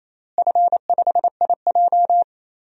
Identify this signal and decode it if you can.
Morse code, heard as a keyed tone, about 28 wpm, 700 Hz. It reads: F5IJ
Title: DXCC-100 Call Signs - 4 Characters - Repeated 2x - 28wpm